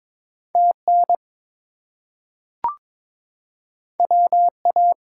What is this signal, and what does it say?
Morse code, heard as a keyed tone, about 22 wpm, 700 Hz. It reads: TN WA
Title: US States and Canadian Provinces 22wpm